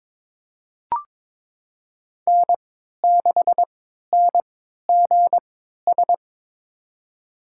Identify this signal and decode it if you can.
Morse code, heard as a keyed tone, about 22 wpm, spaced as for 13 wpm, 700 Hz. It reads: N6NGS
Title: USA Call Signs - 5 Characters - 3x Character Spacing - 22wpm